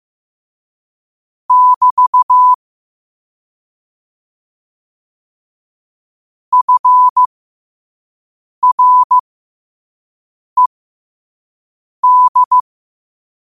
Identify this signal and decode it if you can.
Morse code, heard as a keyed tone, about 15 wpm, spaced as for 5 wpm, 1000 Hz. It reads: <BT> FRED
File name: qotdfiles / 5wpm